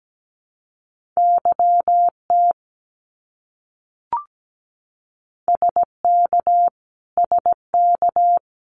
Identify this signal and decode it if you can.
Morse code, heard as a keyed tone, about 17 wpm, 700 Hz. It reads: YT SK SK
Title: Canadian Provinces and Territories - Repeated 3x - 17wpm